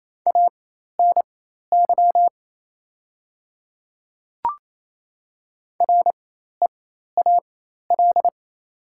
Morse code, heard as a keyed tone, about 28 wpm, spaced as for 13 wpm, 700 Hz. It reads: ANY REAL
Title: 4x Character Spacing - Top 500 Words - 28wpm